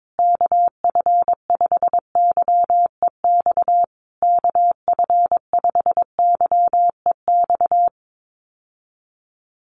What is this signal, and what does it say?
Morse code, heard as a keyed tone, about 22 wpm, 700 Hz. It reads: KF5YEX KF5YEX
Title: USA Call Signs Repeated 3x 22wpm